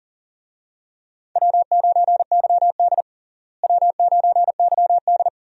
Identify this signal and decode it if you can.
Morse code, heard as a keyed tone, about 40 wpm, 700 Hz. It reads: W9YD W9YD
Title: USA Call Signs - 4 Characters - Repeated 3x - 40wpm